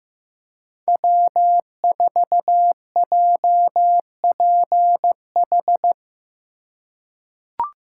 Morse code, heard as a keyed tone, about 15 wpm, 700 Hz. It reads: W4JPH
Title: K1USN SST QSO Practice - 15wpm